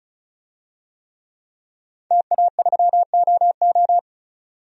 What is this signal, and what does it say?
Morse code, heard as a keyed tone, about 36 wpm, 700 Hz. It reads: TA3OO
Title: DXCC-100 Call Signs - 5 Characters - 35wpm